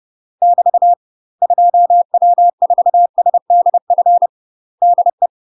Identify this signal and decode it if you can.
Morse code, heard as a keyed tone, about 30 wpm, 700 Hz. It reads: <BT> 2W4SDF DE
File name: UK MASTER QSOs TRAINING / 30WPM / UK MASTER QSOs 0032